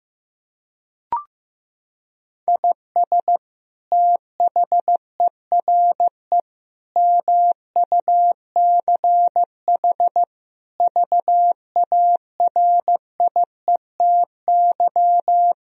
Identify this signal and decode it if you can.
Morse code, heard as a keyed tone, about 15 wpm, 700 Hz. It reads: IS THERE MUCH VARIETY
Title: Sentences from Top 1000 Words - Review Part 4 - 15wpm